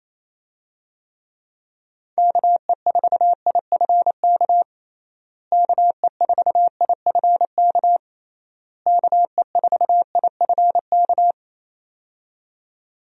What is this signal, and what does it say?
Morse code, heard as a keyed tone, about 28 wpm, 700 Hz. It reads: KE4IFK KE4IFK KE4IFK
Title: USA Call Signs - 6 Characters - Repeated 3x - 28wpm